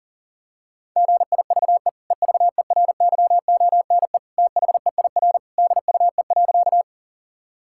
Morse code, heard as a keyed tone, about 40 wpm, 700 Hz. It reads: GIVE EVERYONE THEIR DUE.
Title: Sentences from Top 1000 Words - Review Part 6 - 40wpm